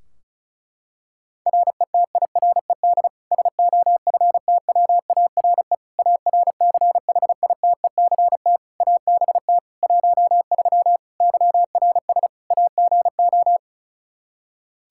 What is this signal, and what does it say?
Morse code, heard as a keyed tone, about 36 wpm, 700 Hz. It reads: RETIRED SOFTWARE ARCHITECT ABT 13 YRS AGO
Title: Practice QSO - Exchange 3 - More Info - 35wpm